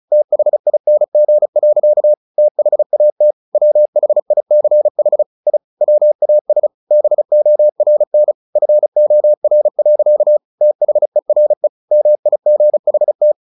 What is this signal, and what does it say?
Morse code, heard as a keyed tone, about 34 wpm, 600 Hz. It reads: THING. THAT WHICH I WAS BORN FOR. THERE MIGHT
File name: the old man and the sea / the old man and the sea 1 35wpm